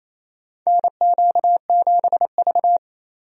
Morse code, heard as a keyed tone, about 28 wpm, 700 Hz. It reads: NQ7V
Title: USA Call Signs - 4 Characters - 28wpm